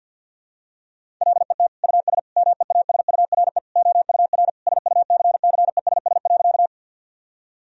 Morse code, heard as a keyed tone, about 47 wpm, 700 Hz. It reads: LET US MEASURE OUR SUCCESS.
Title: Sentences from Top 1000 Words - Review Part 2 - 50wpm